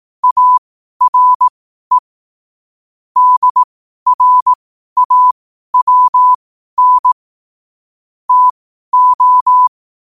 Morse code, heard as a keyed tone, about 18 wpm, spaced as for 12 wpm, 1000 Hz. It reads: ARE DRAWN TO
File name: qotdfiles / 12wpm